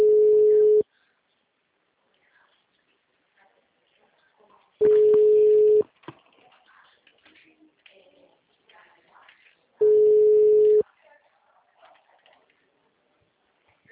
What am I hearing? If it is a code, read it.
Ringback tone (Europe, 425 Hz), with speech.
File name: out- -2002- - - .10241